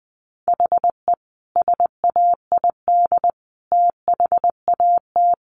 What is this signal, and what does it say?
Morse code, heard as a keyed tone, about 20 wpm, 700 Hz. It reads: HE SAID THAT
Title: Sentences from Top 800 Words 20wpm